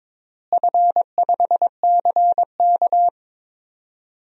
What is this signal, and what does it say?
Morse code, heard as a keyed tone, about 22 wpm, 700 Hz. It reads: F5CK
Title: DXCC-100 Call Signs - 4 Characters - Repeated 3x - 22wpm